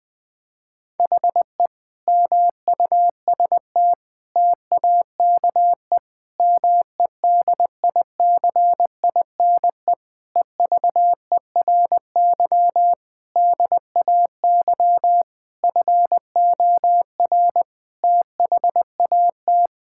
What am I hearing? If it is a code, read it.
Morse code, heard as a keyed tone, about 20 wpm, 700 Hz. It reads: HE MUST TAKE MEDICINE EVERY DAY FOR THAT
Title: Sentences from Top 1000 Words - Review Part 6 - 20wpm